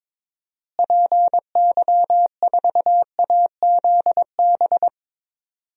Morse code, heard as a keyed tone, about 22 wpm, 700 Hz. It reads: PY4AZB
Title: DXCC-100 Call Signs - 6 Characters - Repeated 3x - 22wpm